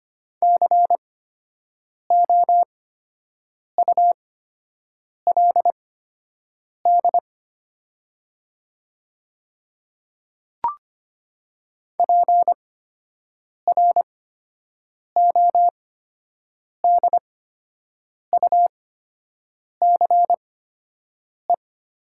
Morse code, heard as a keyed tone, about 25 wpm, spaced as for 7 wpm, 700 Hz. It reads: COULD PRODUCE
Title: 8x Character Spacing - Top 500 Words - 25wpm